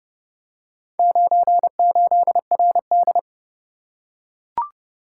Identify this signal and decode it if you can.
Morse code, heard as a keyed tone, about 26 wpm, 700 Hz. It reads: 98RD